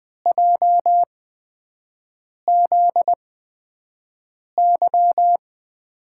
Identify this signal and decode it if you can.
Morse code, heard as a keyed tone, about 20 wpm, spaced as for 5 wpm, 700 Hz. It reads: JZY
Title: USA Call Signs - 5 Characters - 8x Character Spacing - 20wpm